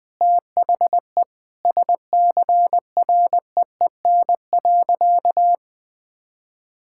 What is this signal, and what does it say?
Morse code, heard as a keyed tone, about 20 wpm, 700 Hz. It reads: THE SCREEN.
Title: Sentences from Top 1000 Words - Review Part 6 - 20wpm